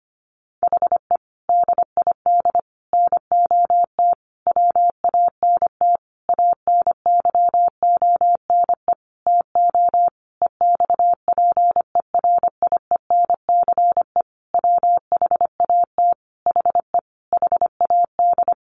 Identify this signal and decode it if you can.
Morse code, heard as a keyed tone, about 25 wpm, 700 Hz. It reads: HE DID NOT WANT ANYONE TO EXPERIENCE WHAT HE HAD
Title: Sentences from Top 1100 Words 25wpm